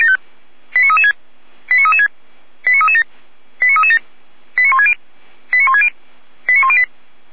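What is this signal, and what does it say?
Five-tone selective call (ZVEI), heard as quick runs of five tones, tones of about 70 ms. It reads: 89336 89336 89337 89337 89160 89160 89108
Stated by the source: Received on SDR, ZVEI five-tone code 89165.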